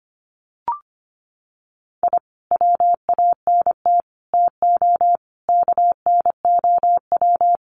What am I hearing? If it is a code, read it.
Morse code, heard as a keyed tone, about 25 wpm, 700 Hz. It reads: I WANT TO KNOW
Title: Sets of 5 Words Encore2 25wpm